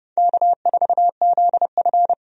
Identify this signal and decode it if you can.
Morse code, heard as a keyed tone, about 30 wpm, 700 Hz. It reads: K4ZF
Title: USA Call Signs - 4 Characters - 30wpm